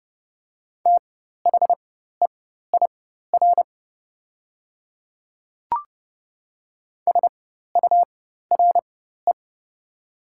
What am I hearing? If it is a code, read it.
Morse code, heard as a keyed tone, about 30 wpm, spaced as for 14 wpm, 700 Hz. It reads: THEIR SURE
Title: 4x Character Spacing - Top 300 Words - 30wpm